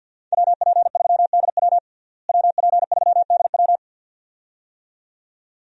Morse code, heard as a keyed tone, about 49 wpm, 700 Hz. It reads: WP3DW WP3DW
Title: USA Call Signs - 5 Characters - Repeated 2x - 50wpm